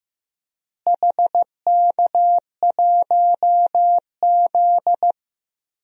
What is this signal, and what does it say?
Morse code, heard as a keyed tone, about 15 wpm, 700 Hz. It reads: HK1Z
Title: DXCC-100 Call Signs - 4 Characters - Repeated 3x - 15wpm